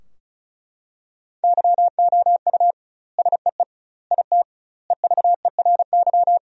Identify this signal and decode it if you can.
Morse code, heard as a keyed tone, about 36 wpm, 700 Hz. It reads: YOU SEE IT EVERY
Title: 2x Word-Spacing - Sentences from Top 200 Words - 35wpm